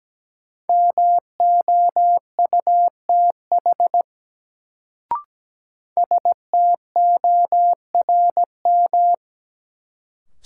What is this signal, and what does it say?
Morse code, heard as a keyed tone, about 17 wpm, 700 Hz. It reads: MOUTH STORM
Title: Sentences from Top 1500 Words - New Word Drill Alt - 17wpm